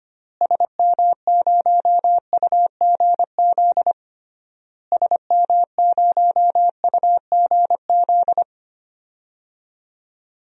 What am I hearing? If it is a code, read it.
Morse code, heard as a keyed tone, about 25 wpm, 700 Hz. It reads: SM0UGZ SM0UGZ
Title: DXCC-100 Call Signs - 6 Characters - Repeated 3x - 25wpm